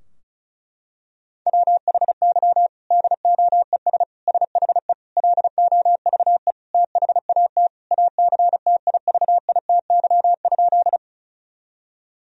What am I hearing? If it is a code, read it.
Morse code, heard as a keyed tone, about 34 wpm, 700 Hz. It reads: WHY DOES SHE LOVE THAT ACTIVITY?
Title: Sentences from Top 1000 Words - Review Part 4 - 35wpm